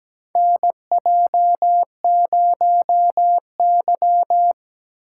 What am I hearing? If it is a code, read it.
Morse code, heard as a keyed tone, about 17 wpm, 700 Hz. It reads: NJ0Y
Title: USA Call Signs - 4 Characters - 4x Character Spacing - 17wpm